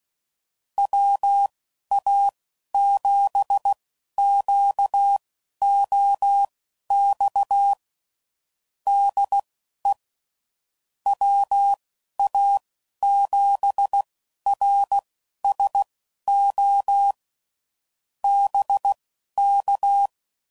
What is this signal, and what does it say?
Morse code, heard as a keyed tone, about 16 wpm, spaced as for 11 wpm, 800 Hz. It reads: WA7QOX DE WA7RSO BK
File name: CW Code Groups @10WPM / QSOs / QSO WA7QOX